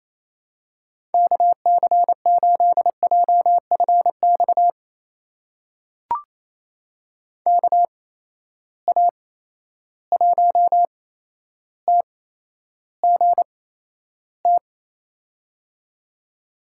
Morse code, heard as a keyed tone, about 28 wpm, spaced as for 12 wpm, 700 Hz. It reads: KC8JFX KA1TGT
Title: USA Call Signs - 6 Characters - 8x Character Spacing - 28wpm